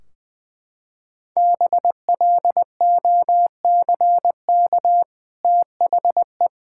Morse code, heard as a keyed tone, about 20 wpm, 700 Hz. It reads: BLOCK THE
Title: Sentences from Top 1600 Words 20wpm